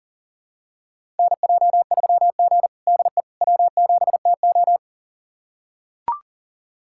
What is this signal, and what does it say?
Morse code, heard as a keyed tone, about 40 wpm, 700 Hz. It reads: NJ3G DE W7TO
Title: USA Callsign DE Callsign - 4 Characters - 40wpm